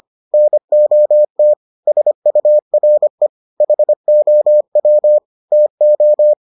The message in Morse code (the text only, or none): NOT SURE HOW TO